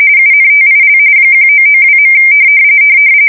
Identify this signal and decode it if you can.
RTTY, about 45.45 baud. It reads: UNTIL 1992.
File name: fsk170 / battleshipnj